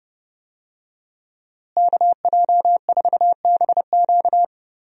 Morse code, heard as a keyed tone, about 32 wpm, 700 Hz. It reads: KJ4BQ
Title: USA Call Signs - 5 Characters - Repeated 2x - 30wpm